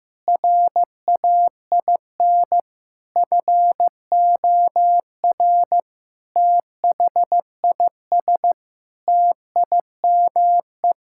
Morse code, heard as a keyed tone, about 15 wpm, 700 Hz. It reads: RAIN FOR THIS TIME
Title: Sentences from Top 1000 Words - Review Part 6 - 15wpm